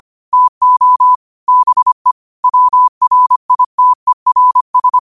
Morse code, heard as a keyed tone, about 25 wpm, 1000 Hz. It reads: TO BE WRITERS
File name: qotdfiles / 25wpm